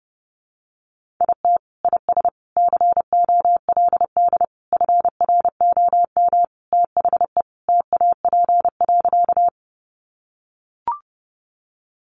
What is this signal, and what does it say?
Morse code, heard as a keyed tone, about 30 wpm, 700 Hz. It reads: IT IS COLD FROM THE TAP.
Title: Sentences from Top 1400 Words 30wpm